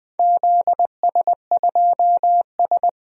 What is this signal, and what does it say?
Morse code, heard as a keyed tone, about 20 wpm, 700 Hz. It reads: ZS2S
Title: DXCC-100 Call Signs - 4 Characters - Repeated 2x - 20wpm